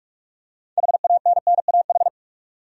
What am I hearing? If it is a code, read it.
Morse code, heard as a keyed tone, about 44 wpm, 700 Hz. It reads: HANNAH
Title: USA Common Names - 45wpm